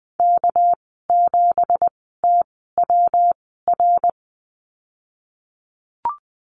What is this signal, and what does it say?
Morse code, heard as a keyed tone, about 20 wpm, spaced as for 14 wpm, 700 Hz. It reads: K7TWR
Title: 231 Lesson 37g - Letter x - Call Signs 20wpm